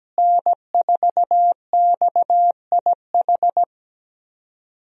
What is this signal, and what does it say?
Morse code, heard as a keyed tone, about 17 wpm, 700 Hz. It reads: N4XIH
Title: USA Call Signs - 5 Characters - Repeated 2x - 17wpm